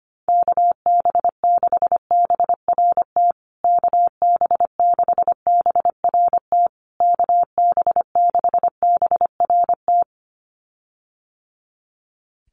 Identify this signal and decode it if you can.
Morse code, heard as a keyed tone, about 25 wpm, 700 Hz. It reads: KB6BRT KB6BRT KB6BRT
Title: USA Call Signs Repeated 3x 25wpm